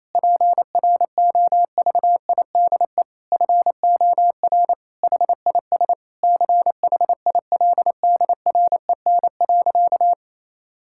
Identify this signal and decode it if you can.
Morse code, heard as a keyed tone, about 28 wpm, 700 Hz. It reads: PROVIDE FOR HIS CHILDREN.
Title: Sentences from Top 1600 Words 28wpm